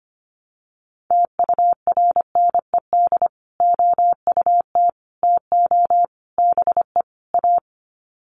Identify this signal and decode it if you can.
Morse code, heard as a keyed tone, about 25 wpm, 700 Hz. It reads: TURNED OUT TO BE A
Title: Sets of 5 Words 25wpm